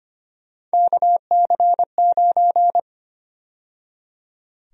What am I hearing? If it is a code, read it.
Morse code, heard as a keyed tone, about 25 wpm, 700 Hz. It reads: KC9
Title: USA Call Sign Prefixes - 25wpm